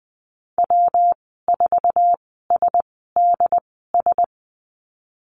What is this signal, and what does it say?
Morse code, heard as a keyed tone, about 20 wpm, spaced as for 15 wpm, 700 Hz. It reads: W4SDS
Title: 140 Lesson 23g - Letter f - Call Signs 20wpm